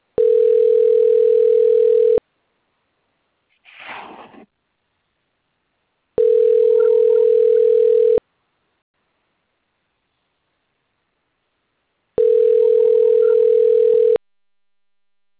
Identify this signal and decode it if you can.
Ringback tone (North America), 440 + 480 Hz.